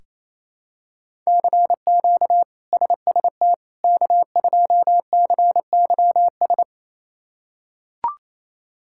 Morse code, heard as a keyed tone, about 28 wpm, 700 Hz. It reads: CQ SST K2CYS
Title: K1USN SST QSO Practice - 28wpm